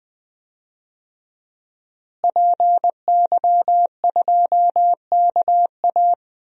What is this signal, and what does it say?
Morse code, heard as a keyed tone, about 20 wpm, 700 Hz. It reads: PY2KA
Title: DXCC-100 Call Signs - 5 Characters - Repeated 2x - 20wpm